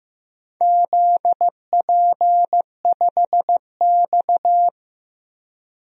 Morse code, heard as a keyed tone, about 16 wpm, 700 Hz. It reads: ZP5X
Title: DXCC-100 Call Signs - 4 Characters - Repeated 2x - 15wpm